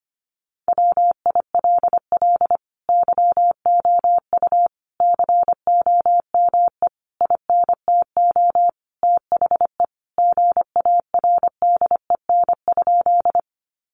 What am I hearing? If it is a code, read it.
Morse code, heard as a keyed tone, about 25 wpm, 700 Hz. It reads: WILL YOU COME INTO THE GARDEN?
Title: Sentences from Top 700 Words 25wpm